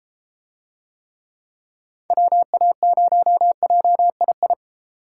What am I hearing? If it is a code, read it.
Morse code, heard as a keyed tone, about 32 wpm, 700 Hz. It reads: WA0JII